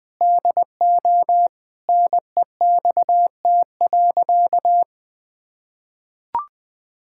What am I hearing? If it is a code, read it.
Morse code, heard as a keyed tone, about 20 wpm, 700 Hz. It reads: DO NEXT.
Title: Sentences with 6-Word Phrases - Spoken Beginning - 20wpm